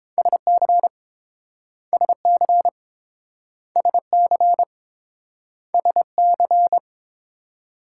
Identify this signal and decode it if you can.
Morse code, heard as a keyed tone, about 30 wpm, 700 Hz. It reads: SC SC SC SC